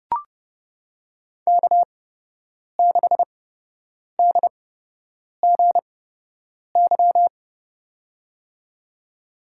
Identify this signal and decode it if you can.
Morse code, heard as a keyed tone, about 30 wpm, spaced as for 8 wpm, 700 Hz. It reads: K6DGY
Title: USA Call Signs - 5 Characters - 8x Character Spacing - 30wpm